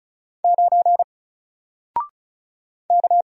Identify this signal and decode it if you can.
Morse code, heard as a keyed tone, about 36 wpm, 700 Hz. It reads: 9 K
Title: Single Letter-Number 35wpm